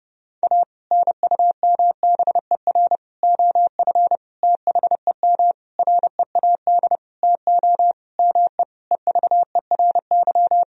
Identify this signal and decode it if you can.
Morse code, heard as a keyed tone, about 32 wpm, 700 Hz. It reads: A NUMBER OF THEM READ TO ME EVERY
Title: Sentences from Top 1000 Words - Review Part 1 - 30wpm